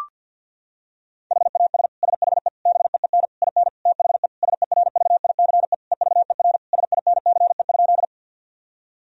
Morse code, heard as a keyed tone, about 50 wpm, 700 Hz. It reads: HAS SHE BEEN IN THE SERVICE EVER SINCE?
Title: Sentences from Top 1000 Words - Review Part 4 - 50wpm